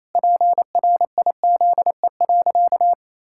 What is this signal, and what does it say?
Morse code, heard as a keyed tone, about 28 wpm, 700 Hz. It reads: PRIZE.
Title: Pangram Challenge v2 - 28wpm